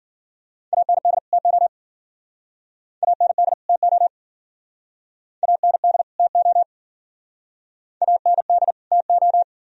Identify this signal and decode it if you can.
Morse code, heard as a keyed tone, about 50 wpm, 700 Hz. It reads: AND TO AND TO AND TO AND TO